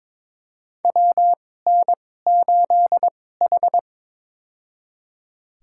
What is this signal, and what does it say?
Morse code, heard as a keyed tone, about 22 wpm, spaced as for 16 wpm, 700 Hz. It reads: WN8H
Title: USA Call Signs - 4 Characters - 2x Character Spacing - 22wpm